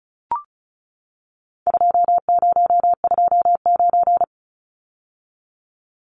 Morse code, heard as a keyed tone, about 34 wpm, 700 Hz. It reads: 2029
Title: Numbers - 4-Digit 35wpm